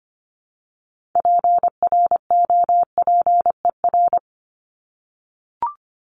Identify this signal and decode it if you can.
Morse code, heard as a keyed tone, about 25 wpm, 700 Hz. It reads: PROPER
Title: LY Suffix - Top 100 Words with Base Words 25wpm